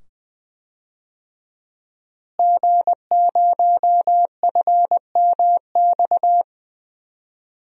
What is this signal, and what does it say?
Morse code, heard as a keyed tone, about 20 wpm, 700 Hz. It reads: G0FMX
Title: DXCC-100 Call Signs - 5 Characters - Repeated 2x - 20wpm